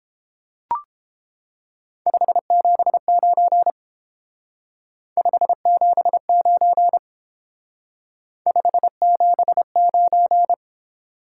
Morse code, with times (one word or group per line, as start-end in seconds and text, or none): 2.1-3.7 579
5.2-7.0 579
8.5-10.5 579